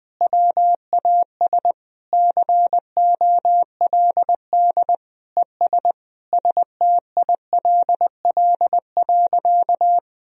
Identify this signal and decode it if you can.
Morse code, heard as a keyed tone, about 21 wpm, 700 Hz. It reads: WAS COLD ES STILL.
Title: Sentences from Top 400 Words and Abbreviations 20wpm